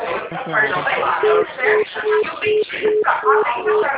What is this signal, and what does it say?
Congestion tone (425 Hz), with speech.